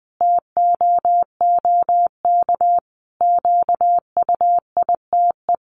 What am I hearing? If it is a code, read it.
Morse code, heard as a keyed tone, about 20 wpm, 700 Hz. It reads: TOOK QUITE
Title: Sentences from Top 1200 Words 20wpm